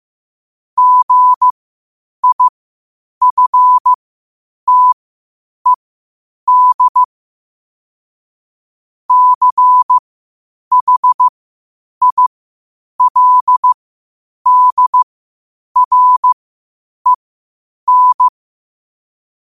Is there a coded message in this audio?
Yes: Morse code from 0.8 to 18.3 s, heard as a keyed tone, about 15 wpm, spaced as for 8 wpm, 1000 Hz. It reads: GIFTED CHILDREN